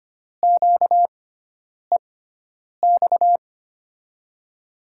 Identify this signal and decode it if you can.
Morse code, heard as a keyed tone, about 25 wpm, spaced as for 9 wpm, 700 Hz. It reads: QEX
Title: USA Call Signs - 6x Character Spacing - 25wpm